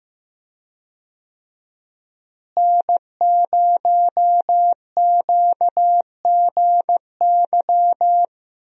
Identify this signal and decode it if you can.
Morse code, heard as a keyed tone, about 15 wpm, 700 Hz. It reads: N0QGY
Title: USA Call Signs - 5 Characters - Repeated 2x - 15wpm